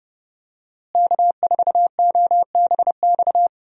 Morse code, heard as a keyed tone, about 32 wpm, 700 Hz. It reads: K4OBX
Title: USA Call Signs - 5 Characters - Repeated 2x - 30wpm